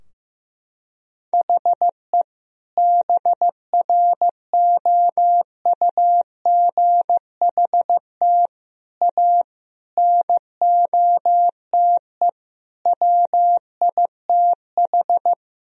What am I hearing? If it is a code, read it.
Morse code, heard as a keyed tone, about 16 wpm, 700 Hz. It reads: HE BROUGHT A NOTE WITH
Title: Sentences from Top 1000 Words - Review Part 2 - 15wpm